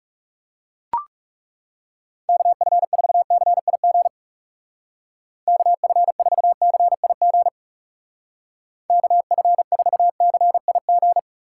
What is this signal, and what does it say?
Morse code, heard as a keyed tone, about 37 wpm, 700 Hz. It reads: KF4CIG KF4CIG KF4CIG
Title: Speed Racing - USA 2x3 Call Signs 30wpm